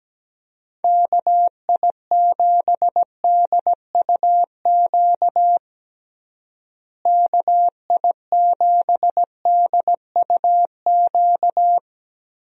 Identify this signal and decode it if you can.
Morse code, heard as a keyed tone, about 17 wpm, 700 Hz. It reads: KI7DUQ KI7DUQ
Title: USA Call Signs - 6 Characters - Repeated 3x - 17wpm